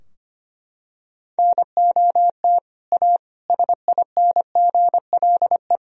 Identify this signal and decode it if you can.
Morse code, heard as a keyed tone, about 25 wpm, 700 Hz. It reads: NOT A SINGLE
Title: Sentences from Top 1000 Words - Review Part 4 - 25wpm